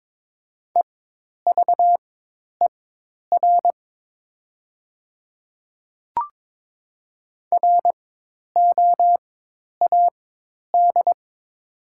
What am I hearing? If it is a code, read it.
Morse code, heard as a keyed tone, about 22 wpm, spaced as for 10 wpm, 700 Hz. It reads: EVER ROAD
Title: 4x Character Spacing - Top 500 Words - 22wpm